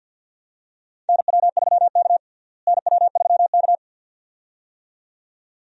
Morse code, heard as a keyed tone, about 50 wpm, 700 Hz. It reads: NW3K NW3K
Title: USA Call Signs - 4 Characters - Repeated 2x - 50wpm